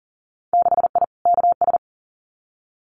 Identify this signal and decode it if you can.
Morse code, heard as a keyed tone, about 40 wpm, 700 Hz. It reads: 6I KS
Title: Winter Field Day Exchanges 40wpm